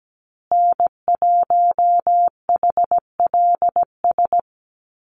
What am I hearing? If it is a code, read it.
Morse code, heard as a keyed tone, about 17 wpm, 700 Hz. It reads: N1HLS
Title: USA Call Signs Repeated 3x 17wpm